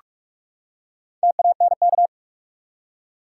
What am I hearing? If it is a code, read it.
Morse code, heard as a keyed tone, about 46 wpm, 700 Hz. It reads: TANK
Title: Four Letter Words - 45wpm